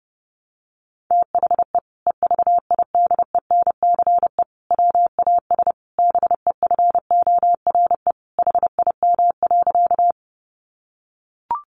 Morse code, heard as a keyed tone, about 30 wpm, 700 Hz. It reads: THE EVIDENCE WAS BEFORE HIM.
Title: Sentences from Top 1300 Words 30wpm